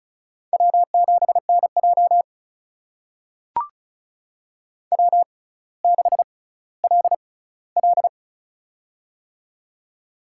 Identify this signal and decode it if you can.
Morse code, heard as a keyed tone, about 35 wpm, spaced as for 18 wpm, 700 Hz. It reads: W7NJ W6LL
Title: USA Call Signs - 6x Character Spacing - 35wpm